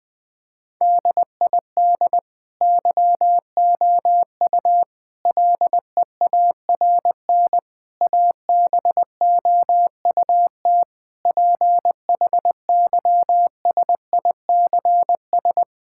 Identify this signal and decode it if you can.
Morse code, heard as a keyed tone, about 20 wpm, 700 Hz. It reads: DID YOU LEARN ABOUT PHYSICS
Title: Sentences from Top 1000 Words - Review Part 5 - 20wpm